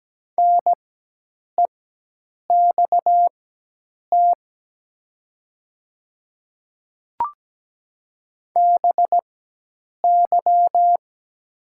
Morse code, heard as a keyed tone, about 17 wpm, spaced as for 8 wpm, 700 Hz. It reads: NEXT BY
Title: 4x Character Spacing - Top 300 Words - 17wpm